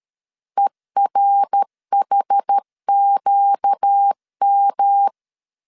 Morse code, heard as a keyed tone, about 13 wpm, 800 Hz. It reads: ERHQM
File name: CW / DET / 1 6 1500C S A 12wpm 3440